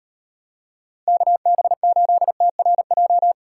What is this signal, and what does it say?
Morse code, heard as a keyed tone, about 38 wpm, 700 Hz. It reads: KB8TRJ